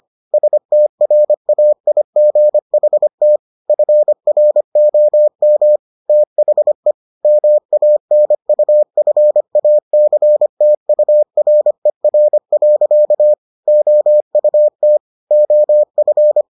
Morse code, heard as a keyed tone, about 25 wpm, 600 Hz. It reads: STRAIGHT FROM THE MANUFACTURER. OUT OF